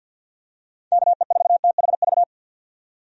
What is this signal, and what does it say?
Morse code, heard as a keyed tone, about 51 wpm, 700 Hz. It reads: KE4THV
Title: USA Call Signs - 6 Characters - 50wpm